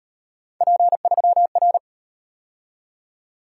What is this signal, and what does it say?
Morse code, heard as a keyed tone, about 34 wpm, 700 Hz. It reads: P3R